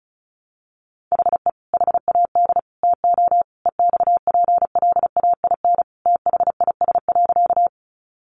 Morse code, heard as a keyed tone, about 34 wpm, 700 Hz. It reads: HE HAD TO EXPLAIN THIS.
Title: Sentences from Top 1100 Words 35wpm